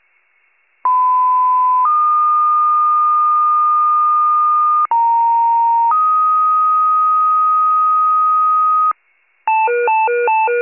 Two-tone page, heard as a one-second tone, then a three-second tone, about 1010 then 1220 Hz, about 930 then 1250 Hz. It is heard from 0.8 to 8.9 s.